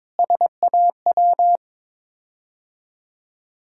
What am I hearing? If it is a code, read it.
Morse code, heard as a keyed tone, about 22 wpm, 700 Hz. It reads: SAW
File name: practice / 22wpm / Sets of 7 Words - 22wpm